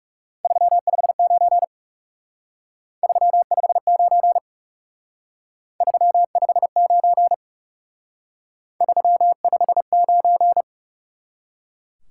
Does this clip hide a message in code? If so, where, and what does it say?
Yes: Morse code from 0.4 to 10.6 s, heard as a keyed tone, about 38 wpm, 700 Hz. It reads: 359 359 359 359